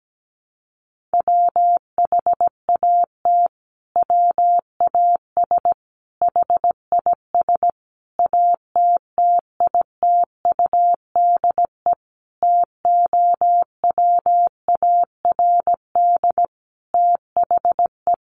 Morse code, heard as a keyed tone, about 17 wpm, 700 Hz. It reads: WHAT WAS HIS ATTITUDE TOWARD THE
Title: Sentences from Top 1300 Words 17wpm